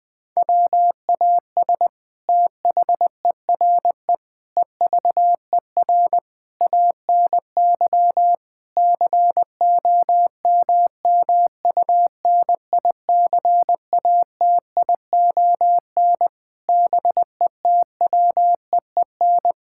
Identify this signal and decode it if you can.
Morse code, heard as a keyed tone, about 20 wpm, 700 Hz. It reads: WAS THERE EVER ANY COMMUNICATION BETWEEN
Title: Sentences from Top 1000 Words - Review Part 6 - 20wpm